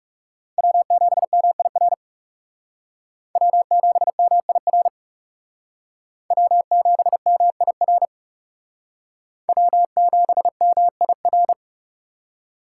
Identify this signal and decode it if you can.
Morse code, heard as a keyed tone, about 37 wpm, 700 Hz. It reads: W7MIR W7MIR W7MIR W7MIR
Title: Speed Racing - USA 1x3 Call Signs 30wpm